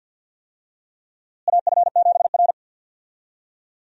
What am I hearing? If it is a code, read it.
Morse code, heard as a keyed tone, about 52 wpm, 700 Hz. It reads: AF7R
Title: USA Call Signs - 4 Characters - Repeated 2x - 50wpm